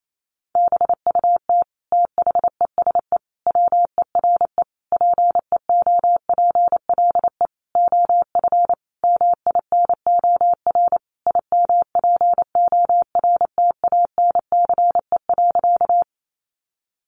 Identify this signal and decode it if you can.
Morse code, heard as a keyed tone, about 28 wpm, 700 Hz. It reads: BUT THESE WERE PEOPLE OF MINOR IMPORTANCE.
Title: Sentences from Top 1500 Words 28wpm